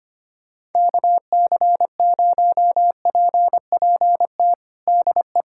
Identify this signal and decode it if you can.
Morse code, heard as a keyed tone, about 25 wpm, 700 Hz. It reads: KC0PPT DE
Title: USA Callsign DE Callsign 25wpm